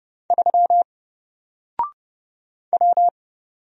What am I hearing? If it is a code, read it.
Morse code, heard as a keyed tone, about 30 wpm, 700 Hz. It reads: 3 W
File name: Single-Letter-Number-30wpm